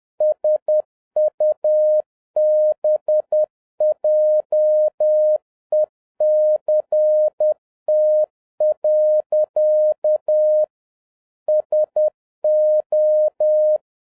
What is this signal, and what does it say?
Morse code, heard as a keyed tone, about 10 wpm, 610 Hz. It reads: SUBJECT. SO